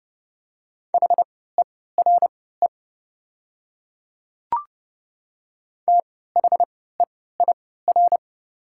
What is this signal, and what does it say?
Morse code, heard as a keyed tone, about 31 wpm, spaced as for 17 wpm, 700 Hz. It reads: HERE THEIR
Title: 3x Character Spacing - Top 200 Words - 30wpm